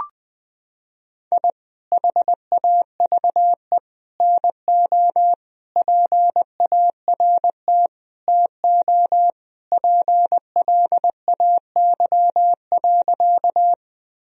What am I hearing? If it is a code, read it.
Morse code, heard as a keyed tone, about 20 wpm, 700 Hz. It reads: I HAVE NO PART TO PLAY.
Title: Sentences from Top 1000 Words - Review Part 1 - 20wpm